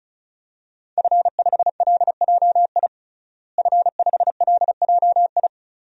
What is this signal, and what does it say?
Morse code, heard as a keyed tone, about 34 wpm, 700 Hz. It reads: F5LJI F5LJI
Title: DXCC-100 Call Signs - 5 Characters - Repeated 3x - 35wpm